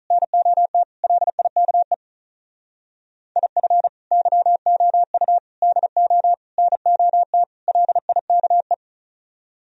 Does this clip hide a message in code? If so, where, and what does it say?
Yes: Morse code from 0.1 to 8.7 s, heard as a keyed tone, about 44 wpm, 700 Hz. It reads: NOT LIKE IF YOU DO NOT LIKE